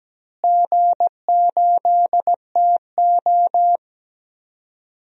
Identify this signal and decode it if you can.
Morse code, heard as a keyed tone, about 17 wpm, 700 Hz. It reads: G8TO
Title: DXCC-100 Call Signs - 4 Characters - Repeated 3x - 17wpm